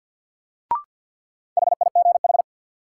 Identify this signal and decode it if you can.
Morse code, heard as a keyed tone, about 53 wpm, 700 Hz. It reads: HIGH
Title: Sets of 2 Words - New Word Drill - 50wpm